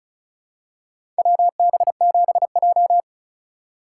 Morse code, heard as a keyed tone, about 36 wpm, 700 Hz. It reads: WB7J
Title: USA Call Signs - 4 Characters - 2x Character Spacing - 35wpm